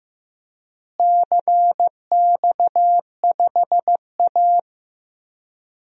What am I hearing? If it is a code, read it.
Morse code, heard as a keyed tone, about 15 wpm, 700 Hz. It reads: CX5A
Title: DXCC-100 Call Signs - 4 Characters - Repeated 3x - 15wpm